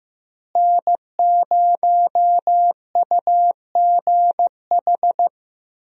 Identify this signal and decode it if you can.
Morse code, heard as a keyed tone, about 15 wpm, 700 Hz. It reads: N0UGH
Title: USA Call Signs - 5 Characters - Repeated 3x - 15wpm